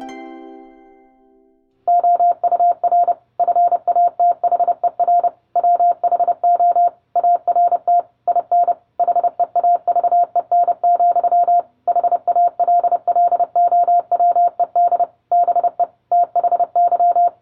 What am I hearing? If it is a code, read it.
Morse code, heard as a keyed tone, about 30 wpm, 700 Hz. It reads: OUR FATHER WHO ART IN HEAVEN, HALLOWED BE THY